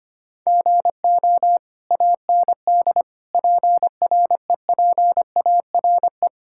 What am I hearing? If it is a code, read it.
Morse code, heard as a keyed tone, about 25 wpm, 700 Hz. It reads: GO AND PREPARE
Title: Sentences from Top 1000 Words - Review Part 6 - 25wpm